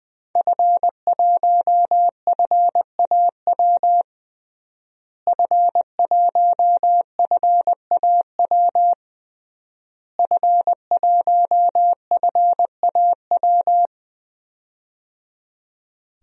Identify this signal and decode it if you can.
Morse code, heard as a keyed tone, about 20 wpm, 700 Hz. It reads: F1FAW F1FAW F1FAW
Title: DXCC-100 Call Signs - 5 Characters - Repeated 3x - 20wpm